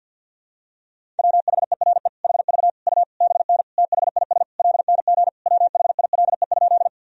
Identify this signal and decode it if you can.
Morse code, heard as a keyed tone, about 52 wpm, 700 Hz. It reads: WHERE HV U BN THIS LNG WHILE?
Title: Sentences from Top 300 Words and Abbreviations 50wpm